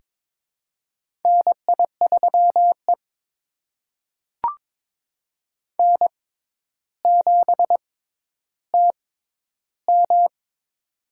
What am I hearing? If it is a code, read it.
Morse code, heard as a keyed tone, about 21 wpm, spaced as for 11 wpm, 700 Hz. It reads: NI3E N7TM
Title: USA Call Signs - 4 Characters - 6x Character Spacing - 22wpm